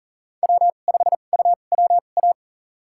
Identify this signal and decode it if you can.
Morse code, heard as a keyed tone, about 40 wpm, spaced as for 29 wpm, 700 Hz. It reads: W5UWA
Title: USA Call Signs - 2x Character Spacing - 40wpm